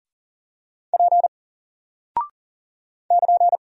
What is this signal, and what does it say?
Morse code, heard as a keyed tone, about 40 wpm, 700 Hz. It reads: P <KN>
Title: Single Character (Letter:Number:Punctuation:Prosign) Alt 40wpm